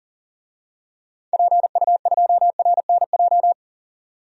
Morse code, heard as a keyed tone, about 40 wpm, 700 Hz. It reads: PU2RNJ
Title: DXCC-100 Call Signs - 6 Characters - 40wpm